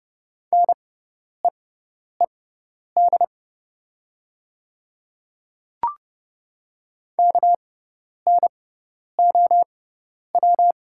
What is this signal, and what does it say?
Morse code, heard as a keyed tone, about 30 wpm, spaced as for 10 wpm, 700 Hz. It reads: NEED KNOW
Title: 6x Character Spacing - Top 300 Words - 30wpm